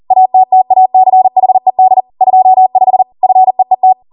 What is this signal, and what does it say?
Morse code, heard as a keyed tone, about 40 wpm, 750 Hz. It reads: ATTACHED 25 FEET